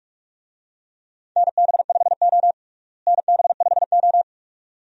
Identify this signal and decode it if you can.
Morse code, heard as a keyed tone, about 46 wpm, 700 Hz. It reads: NB5O NB5O
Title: USA Call Signs - 4 Characters - Repeated 2x - 45wpm